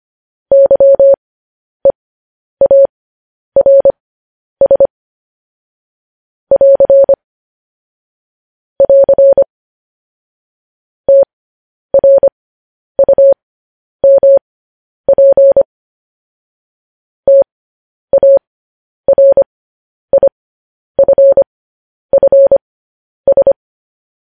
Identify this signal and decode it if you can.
Morse code, heard as a keyed tone, about 25 wpm, spaced as for 10 wpm, 550 Hz. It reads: YEARS <AR> <AR> TRUMP TARIFFS
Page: news headlines in morse code at 10 wpm